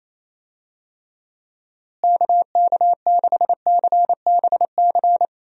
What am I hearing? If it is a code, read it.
Morse code, heard as a keyed tone, about 28 wpm, 700 Hz. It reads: KK6CBC
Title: USA Call Signs - 6 Characters - Repeated 3x - 28wpm